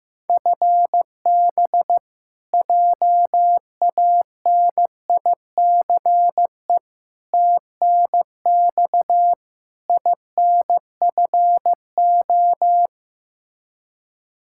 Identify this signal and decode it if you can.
Morse code, heard as a keyed tone, about 15 wpm, 700 Hz. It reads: FB JANICE TNX INFO
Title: K1USN SST QSO Practice 15wpm